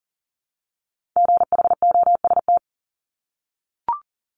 Morse code, heard as a keyed tone, about 39 wpm, 700 Hz. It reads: GHOST
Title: Multi-letter Phonemes Repeated 3x 40wpm